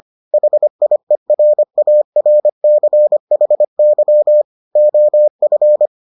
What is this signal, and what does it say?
Morse code, heard as a keyed tone, about 25 wpm, 600 Hz. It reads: HIERARCHY OF